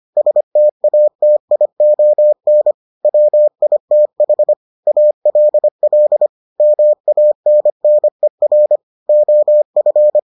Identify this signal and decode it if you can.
Morse code, heard as a keyed tone, about 25 wpm, 600 Hz. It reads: STATION WITH ALL MANNER OF